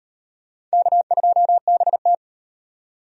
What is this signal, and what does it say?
Morse code, heard as a keyed tone, about 33 wpm, 700 Hz. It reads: K2BT